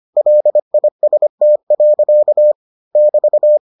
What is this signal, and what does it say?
Morse code, heard as a keyed tone, about 25 wpm, 600 Hz. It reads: LIST. <BT>